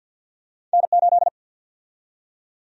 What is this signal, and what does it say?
Morse code, heard as a keyed tone, about 51 wpm, 700 Hz. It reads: N8
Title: USA Call Sign Prefixes - 50wpm